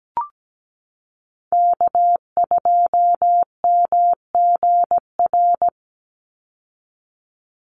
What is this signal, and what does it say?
Morse code, heard as a keyed tone, about 17 wpm, 700 Hz. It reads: K2MGR
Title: Call Signs Repeated 3x - 17wpm